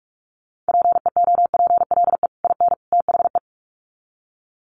Morse code, heard as a keyed tone, about 44 wpm, 700 Hz. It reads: PEOPLE IN THE
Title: Sets of 3 Words 45wpm